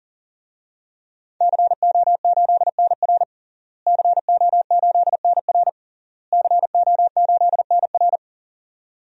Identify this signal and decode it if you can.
Morse code, heard as a keyed tone, about 40 wpm, 700 Hz. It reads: CO8NR CO8NR CO8NR
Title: DXCC-100 Call Signs - 5 Characters - Repeated 3x - 40wpm